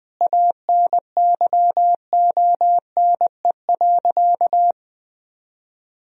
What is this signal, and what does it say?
Morse code, heard as a keyed tone, about 21 wpm, 700 Hz. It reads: ANYONE.
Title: Sentences from Top 1000 Words - Review Part 3 - 20wpm